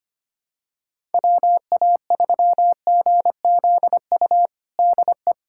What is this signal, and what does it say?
Morse code, heard as a keyed tone, about 25 wpm, 700 Hz. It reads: WA3GZU DE
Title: USA Callsign DE Callsign - 6 Characters - 25wpm